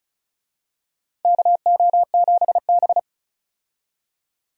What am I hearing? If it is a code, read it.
Morse code, heard as a keyed tone, about 36 wpm, 700 Hz. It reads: KO7B
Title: USA Call Signs - 4 Characters - 35wpm